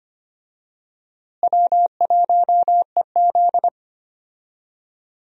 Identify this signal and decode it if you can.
Morse code, heard as a keyed tone, about 25 wpm, 700 Hz. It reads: W1EZ
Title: USA Call Signs - 4 Characters - Repeated 2x - 25wpm